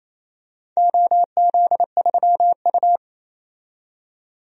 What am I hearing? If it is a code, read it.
Morse code, heard as a keyed tone, about 28 wpm, 700 Hz. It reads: OZ3U
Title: DXCC-100 Call Signs - 4 Characters - 28wpm